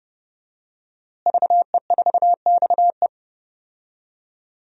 Morse code, heard as a keyed tone, about 30 wpm, 700 Hz. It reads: VE4XE
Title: DXCC-100 Call Signs - 5 Characters - Repeated 2x - 30wpm